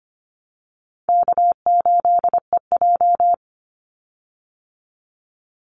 Morse code, heard as a keyed tone, about 25 wpm, 700 Hz. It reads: K8EJ
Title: Call Signs Repeated 3x - 25wpm